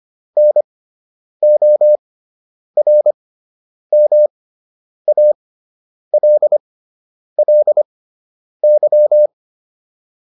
Morse code, heard as a keyed tone, about 25 wpm, spaced as for 9 wpm, 600 Hz. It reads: NORMALLY